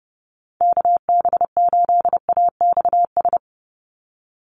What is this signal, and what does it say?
Morse code, heard as a keyed tone, about 30 wpm, 700 Hz. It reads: KB8AXS
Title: USA Call Signs 30wpm